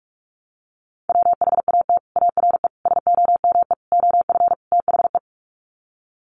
Morse code, heard as a keyed tone, about 44 wpm, 700 Hz. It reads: WHAT ARE SOME OF THE
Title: Sets of 5 Words Encore 45wpm